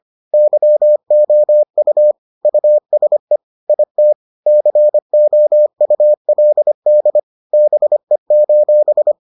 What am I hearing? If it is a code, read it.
Morse code, heard as a keyed tone, about 25 wpm, 600 Hz. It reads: YOU USE IT COULD BE: